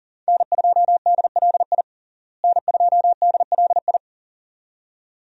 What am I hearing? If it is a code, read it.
Morse code, heard as a keyed tone, about 40 wpm, 700 Hz. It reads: N2DLI N2DLI
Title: USA Call Signs - 5 Characters - Repeated 2x - 40wpm